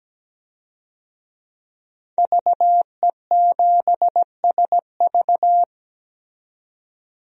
Morse code, heard as a keyed tone, about 17 wpm, 700 Hz. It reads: VE7SV
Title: DXCC-100 Call Signs - 5 Characters - 17wpm